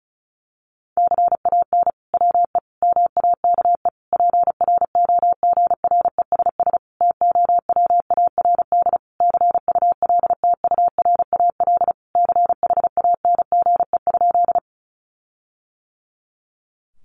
Morse code, heard as a keyed tone, about 34 wpm, 700 Hz. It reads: CAN WE MAKE PROGRESS TOWARD CULTURAL CHANGE?
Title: Sentences from Top 1500 Words 35wpm